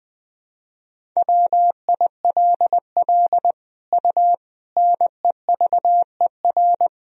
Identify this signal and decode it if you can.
Morse code, heard as a keyed tone, about 20 wpm, 700 Hz. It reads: WILL U NEVER
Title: Sentences from Top 300 Words and Abbreviations 20wpm